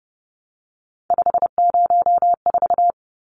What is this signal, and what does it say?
Morse code, heard as a keyed tone, about 30 wpm, 700 Hz. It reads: 504
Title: Numbers - 3-Digit 30wpm